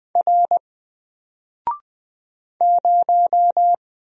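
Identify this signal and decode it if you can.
Morse code, heard as a keyed tone, about 20 wpm, 700 Hz. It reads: R 0
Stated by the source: Single letter and number Morse drill at 20 wpm, from N4LCD.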